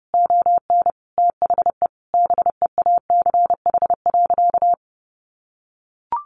Morse code, heard as a keyed tone, about 30 wpm, 700 Hz. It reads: ON THE BEACH.
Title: Sentences from Top 1500 Words 30wpm